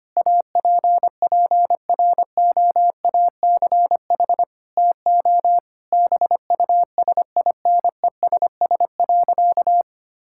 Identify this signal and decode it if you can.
Morse code, heard as a keyed tone, about 25 wpm, 700 Hz. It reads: APPROACH TO BUSINESS.
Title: Sentences from Top 1000 Words - Review Part 6 - 25wpm